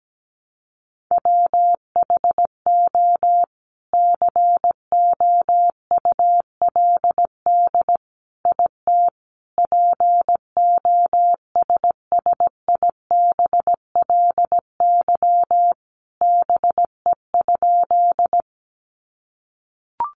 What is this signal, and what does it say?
Morse code, heard as a keyed tone, about 17 wpm, 700 Hz. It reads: WHO COULD IT POSSIBLY BE?
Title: Sentences from Top 1300 Words 17wpm